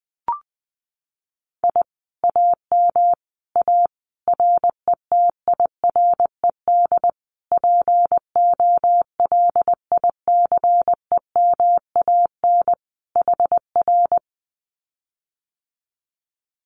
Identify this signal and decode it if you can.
Morse code, heard as a keyed tone, about 20 wpm, 700 Hz. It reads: I AM A RETIRED POLICEMAN HR
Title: Common QSO Phrases Encore 20wpm